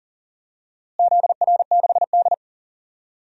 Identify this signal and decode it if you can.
Morse code, heard as a keyed tone, about 40 wpm, 700 Hz. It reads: ZR6D
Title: DXCC-100 Call Signs - 4 Characters - Repeated 2x - 40wpm